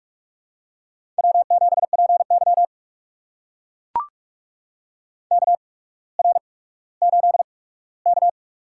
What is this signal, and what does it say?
Morse code, heard as a keyed tone, about 44 wpm, spaced as for 19 wpm, 700 Hz. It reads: W7PY KR8K
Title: USA Call Signs - 4 Characters - 8x Character Spacing - 45wpm